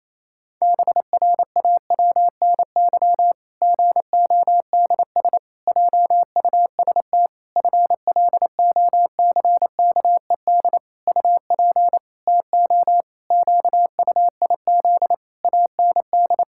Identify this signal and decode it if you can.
Morse code, heard as a keyed tone, about 27 wpm, 700 Hz. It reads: BRAWNY GODS JUST FLOCKED UP TO QUIZ AND
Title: Pangram Challenge v2 - 28wpm